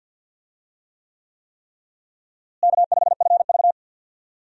Morse code, heard as a keyed tone, about 51 wpm, 700 Hz. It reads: K5FV
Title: USA Call Signs - 4 Characters - Repeated 3x - 50wpm